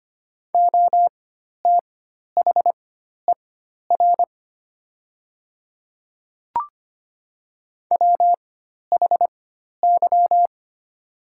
Morse code, heard as a keyed tone, about 25 wpm, spaced as for 12 wpm, 700 Hz. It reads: OTHER WHY
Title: 4x Character Spacing - Top 200 Words - 25wpm